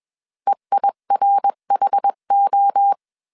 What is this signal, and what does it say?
Morse code, heard as a keyed tone, about 21 wpm, 800 Hz. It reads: EIRHO
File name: CW / DET / 1 3 1500C S B 20wpm 1754